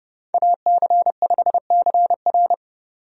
Morse code, heard as a keyed tone, about 30 wpm, 700 Hz. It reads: AC5CR
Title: USA Call Signs - 5 Characters - 30wpm